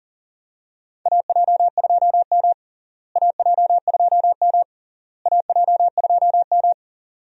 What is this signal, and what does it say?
Morse code, heard as a keyed tone, about 40 wpm, 700 Hz. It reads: AJ2M AJ2M AJ2M
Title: USA Call Signs - 4 Characters - Repeated 3x - 40wpm